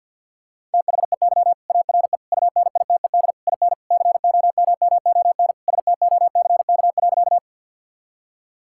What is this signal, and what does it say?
Morse code, heard as a keyed tone, about 50 wpm, 700 Hz. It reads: THEY ARE UNITED IN COMMON STOCK.
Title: Sentences from Top 1000 Words - Review Part 5 - 50wpm